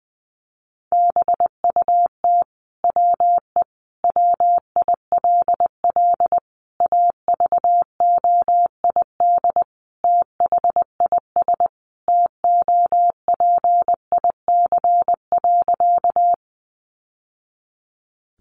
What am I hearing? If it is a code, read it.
Morse code, heard as a keyed tone, about 20 wpm, 700 Hz. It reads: BUT WE WILL AVOID THIS TOPIC.
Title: Sentences from Top 1200 Words 20wpm